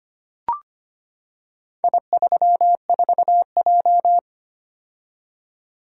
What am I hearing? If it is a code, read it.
Morse code, heard as a keyed tone, about 25 wpm, 700 Hz. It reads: I34J
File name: practice / 25wpm / Four Letter-Number Combinations - 25wpm